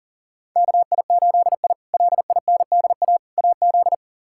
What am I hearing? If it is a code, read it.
Morse code, heard as a keyed tone, about 40 wpm, 700 Hz. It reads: KI8I LINDA AZ
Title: K1USN SST QSO Practice - 40wpm